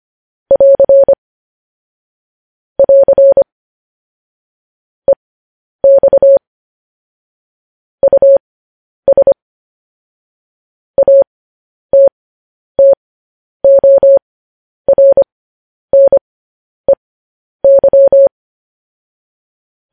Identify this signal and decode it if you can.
Morse code, heard as a keyed tone, about 25 wpm, spaced as for 10 wpm, 550 Hz. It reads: <AR> <AR> EX US ATTORNEY
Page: news-headlines-in-morse-code-at-10-wpm